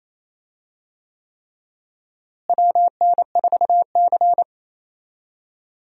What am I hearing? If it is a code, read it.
Morse code, heard as a keyed tone, about 28 wpm, 700 Hz. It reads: WN4C
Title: USA Call Signs - 4 Characters - Repeated 3x - 28wpm